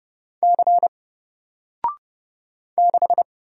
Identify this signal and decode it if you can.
Morse code, heard as a keyed tone, about 30 wpm, 700 Hz. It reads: C 6
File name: Single-Letter-Number-30wpm